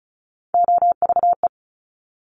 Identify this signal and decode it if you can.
Morse code, heard as a keyed tone, about 36 wpm, 700 Hz. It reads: OVE
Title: Top Three Letter Combinations in Words 35wpm